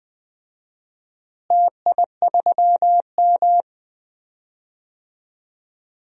Morse code, heard as a keyed tone, about 20 wpm, 700 Hz. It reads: TI3M
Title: DXCC-100 Call Signs - 4 Characters - Repeated 2x - 20wpm